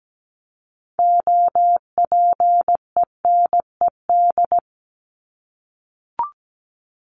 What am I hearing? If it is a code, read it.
Morse code, heard as a keyed tone, about 17 wpm, 700 Hz. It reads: OPENED
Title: ED Suffix - Top 100 Words with Base Words 17wpm